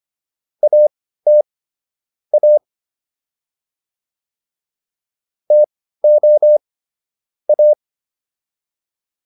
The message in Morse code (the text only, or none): AT A TO A